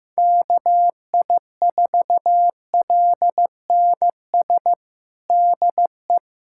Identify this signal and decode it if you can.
Morse code, heard as a keyed tone, about 15 wpm, 700 Hz. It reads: KI4LNS DE
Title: USA Callsign DE Callsign 15wpm